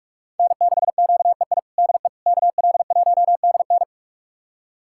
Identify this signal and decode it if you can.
Morse code, heard as a keyed tone, about 41 wpm, 700 Hz. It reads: N6QEI DE KL1DN
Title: USA Callsign DE Callsign - 5 Characters - 45wpm